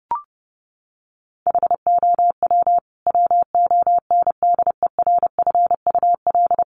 Morse code, heard as a keyed tone, about 30 wpm, 700 Hz. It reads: HOW WONDERFUL
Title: Sentences from Top 1100 Words 30wpm